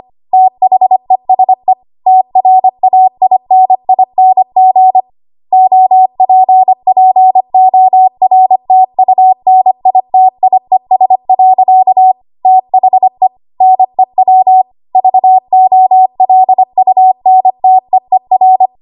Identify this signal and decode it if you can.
Morse code, heard as a keyed tone, about 25 wpm, 750 Hz. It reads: THESE TRAINING OPPORTUNITIES. THE NEW VOLUNTEER
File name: Archive / 25 WPM / 25WPM